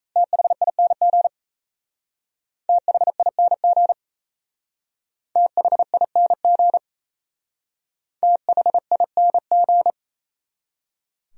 Morse code, heard as a keyed tone, about 34 wpm, 700 Hz. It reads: THING THING THING THING
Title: Speed-Racing Top 300 Words 28wpm